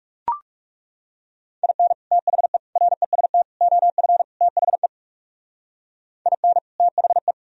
Morse code, heard as a keyed tone, about 38 wpm, 700 Hz. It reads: IN THE REST OF THE IN THE